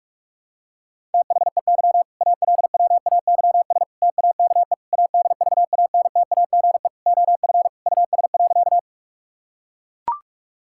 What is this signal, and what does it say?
Morse code, heard as a keyed tone, about 41 wpm, 700 Hz. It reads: THEY ALWAYS TAKE ADVANTAGE OF US.
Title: Sentences from Top 1000 Words - Review Part 6 - 45wpm